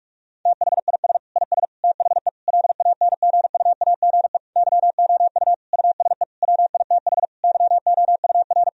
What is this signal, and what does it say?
Morse code, heard as a keyed tone, about 46 wpm, 700 Hz. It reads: THIS IS THE LANGUAGE YOU USE WITH YOUR
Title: Sentences from Top 1000 Words - Review Part 3 - 45wpm